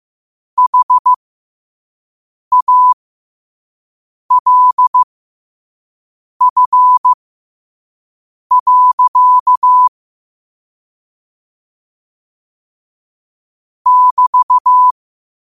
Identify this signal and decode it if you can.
Morse code, heard as a keyed tone, about 15 wpm, spaced as for 5 wpm, 1000 Hz. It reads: HALF. <BT>
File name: qotdfiles / 5wpm